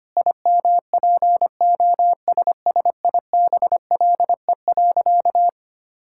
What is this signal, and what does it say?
Morse code, heard as a keyed tone, about 25 wpm, 700 Hz. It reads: IMPOSSIBLE.
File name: practice / 25wpm / Sentences with 4-Word Phrases - 25wpm